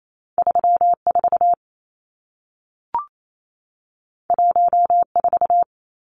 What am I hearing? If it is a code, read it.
Morse code, heard as a keyed tone, about 28 wpm, 700 Hz. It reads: 34 14
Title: Numbers - 2-Digit 28wpm